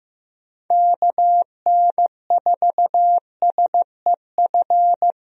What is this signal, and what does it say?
Morse code, heard as a keyed tone, about 15 wpm, 700 Hz. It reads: KN4SEF
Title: USA Call Signs - 6 Characters - Repeated 2x - 15wpm